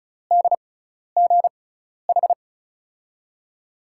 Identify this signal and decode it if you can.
Morse code, heard as a keyed tone, about 36 wpm, spaced as for 12 wpm, 700 Hz. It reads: DGH
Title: USA Call Signs - 6 Characters - 6x Character Spacing - 35wpm